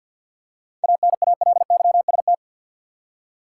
Morse code, heard as a keyed tone, about 51 wpm, 700 Hz. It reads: ANALYST
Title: Seven Letter Words - 50wpm